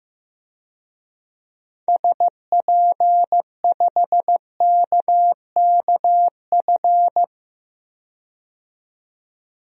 Morse code, heard as a keyed tone, about 15 wpm, 700 Hz. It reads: SP5KKF
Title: DXCC-100 Call Signs - 6 Characters - 15wpm